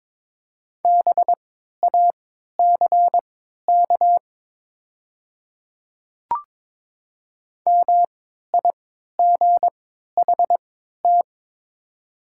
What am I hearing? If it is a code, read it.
Morse code, heard as a keyed tone, about 22 wpm, spaced as for 12 wpm, 700 Hz. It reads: BACK MIGHT
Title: 3x Character Spacing - Top 500 Words - 22wpm